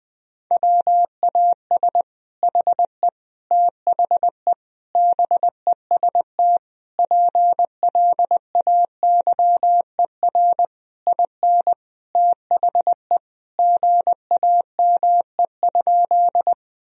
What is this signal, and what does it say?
Morse code, heard as a keyed tone, about 20 wpm, 700 Hz. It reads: WAS HE THE BEST PLAYER IN THE GAME?
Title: Sentences from Top 1000 Words - Review Part 4 - 20wpm